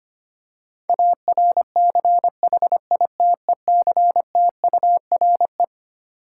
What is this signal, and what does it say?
Morse code, heard as a keyed tone, about 25 wpm, 700 Hz. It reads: ARCHITECTURE
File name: Twelve Letter Words - 25wpm